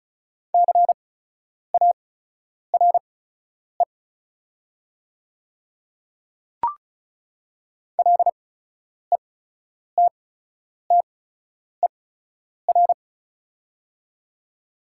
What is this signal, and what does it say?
Morse code, heard as a keyed tone, about 36 wpm, spaced as for 10 wpm, 700 Hz. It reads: CARE LETTER
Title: 8x Character Spacing - Top 300 Words - 35wpm